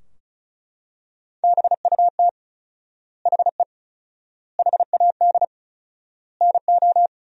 Morse code, heard as a keyed tone, about 36 wpm, 700 Hz. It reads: BUT HE HAD NO
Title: 4x Word-Spacing - Sentences from Top 100 Words - 35wpm